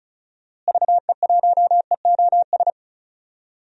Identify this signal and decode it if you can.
Morse code, heard as a keyed tone, about 34 wpm, 700 Hz. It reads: VE1EOS
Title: DXCC-100 Call Signs - 6 Characters - Repeated 3x - 35wpm